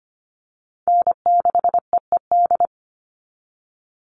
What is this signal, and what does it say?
Morse code, heard as a keyed tone, about 25 wpm, 700 Hz. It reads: N6EED
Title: Call Signs - 25wpm